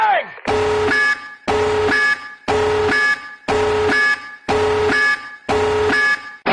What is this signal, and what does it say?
Busy tone (Europe, 425 Hz), with speech.